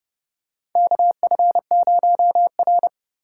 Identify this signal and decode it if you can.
Morse code, heard as a keyed tone, about 30 wpm, 700 Hz. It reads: KF0R
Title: USA Call Signs - 4 Characters - 2x Character Spacing - 30wpm